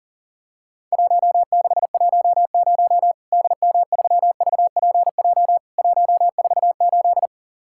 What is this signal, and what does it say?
Morse code, heard as a keyed tone, about 40 wpm, 700 Hz. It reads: 1610 DM3VPJ 148
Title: Worked All Europe DX Contest - QTC Exchange - 40wpm